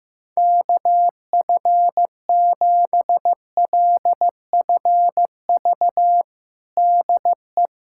Morse code, heard as a keyed tone, about 15 wpm, 700 Hz. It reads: KF7LFV DE
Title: USA Callsign DE Callsign 15wpm